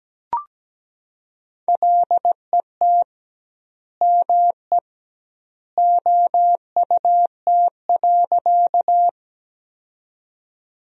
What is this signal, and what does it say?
Morse code, heard as a keyed tone, about 17 wpm, 700 Hz. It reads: LET ME OUT.
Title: 2x Word-Spacing - Sentences from Top 300 Words - 17wpm